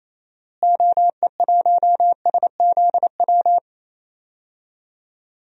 Morse code, heard as a keyed tone, about 28 wpm, 700 Hz. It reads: OE1SZW
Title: DXCC-100 Call Signs - 6 Characters - Repeated 3x - 28wpm